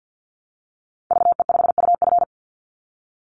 Morse code, heard as a keyed tone, about 50 wpm, 700 Hz. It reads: VE5UF
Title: Call Signs Repeated 2x - 50wpm